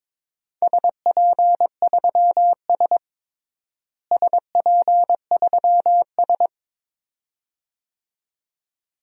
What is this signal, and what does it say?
Morse code, heard as a keyed tone, about 22 wpm, 700 Hz. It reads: SP3S SP3S
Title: DXCC-100 Call Signs - 4 Characters - Repeated 2x - 22wpm